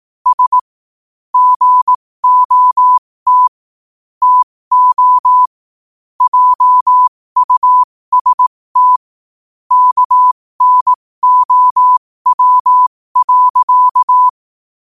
Morse code, heard as a keyed tone, about 18 wpm, spaced as for 15 wpm, 1000 Hz. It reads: S GOT TO JUST KNOW.
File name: qotdfiles / 15wpm